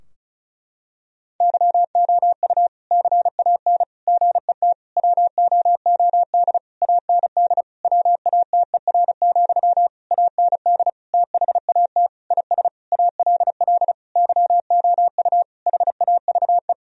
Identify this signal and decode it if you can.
Morse code, heard as a keyed tone, about 36 wpm, 700 Hz. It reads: YOU CAN GET WOOD AND WATER, AND THAT IS ALL YOU HAVE
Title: Sentences from Top 1000 Words - Review Part 2 - 35wpm